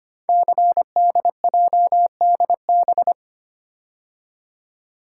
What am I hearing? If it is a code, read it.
Morse code, heard as a keyed tone, about 25 wpm, 700 Hz. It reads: CDJDB
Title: Code Groups - 5 Letters - 25wpm